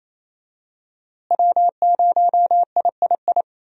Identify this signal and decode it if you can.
Morse code, heard as a keyed tone, about 27 wpm, 700 Hz. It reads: W0III
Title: USA Call Signs - 5 Characters - Repeated 3x - 28wpm